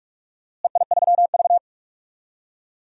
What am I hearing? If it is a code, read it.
Morse code, heard as a keyed tone, about 44 wpm, 700 Hz. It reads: EI3V
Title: DXCC-100 Call Signs - 4 Characters - Repeated 3x - 45wpm